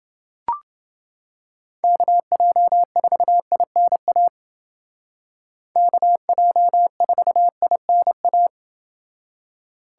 Morse code, heard as a keyed tone, about 29 wpm, 700 Hz. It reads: KJ4INA KJ4INA